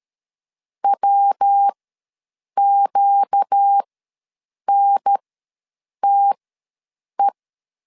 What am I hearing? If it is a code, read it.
Morse code, heard as a keyed tone, about 13 wpm, spaced as for 7 wpm, 800 Hz. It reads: WQNTE
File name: LEZIONI / 5 LBVC / 5 L ws 12wpm 2958